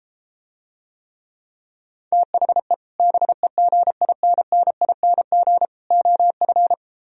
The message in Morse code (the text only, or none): THE BEGINNING OF